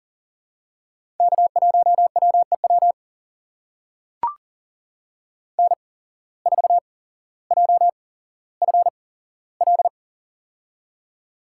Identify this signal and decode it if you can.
Morse code, heard as a keyed tone, about 40 wpm, spaced as for 17 wpm, 700 Hz. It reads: K1WEW N4JFL
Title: USA Call Signs - 5 Characters - 8x Character Spacing - 40wpm